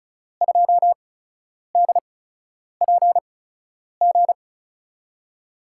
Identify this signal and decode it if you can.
Morse code, heard as a keyed tone, about 36 wpm, spaced as for 10 wpm, 700 Hz. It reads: 2DPG
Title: USA Call Signs - 8x Character Spacing - 35wpm